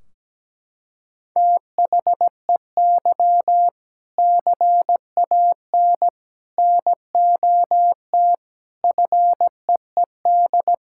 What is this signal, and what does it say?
Morse code, heard as a keyed tone, about 17 wpm, 700 Hz. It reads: THEY CAN NOT FEED
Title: Sentences from Top 1600 Words 17wpm